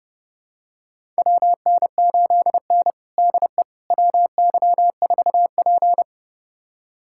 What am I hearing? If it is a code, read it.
Morse code, heard as a keyed tone, about 29 wpm, 700 Hz. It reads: WN8N DE WY4P
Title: USA Callsign DE Callsign - 4 Characters - 30wpm